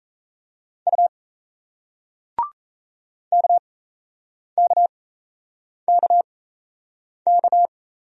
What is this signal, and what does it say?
Morse code, heard as a keyed tone, about 38 wpm, spaced as for 8 wpm, 700 Hz. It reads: U KKKK